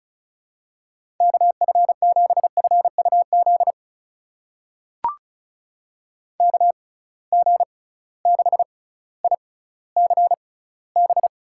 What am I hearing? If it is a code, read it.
Morse code, heard as a keyed tone, about 35 wpm, spaced as for 18 wpm, 700 Hz. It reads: KF7FUZ KG6ICB
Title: USA Call Signs - 6 Characters - 6x Character Spacing - 35wpm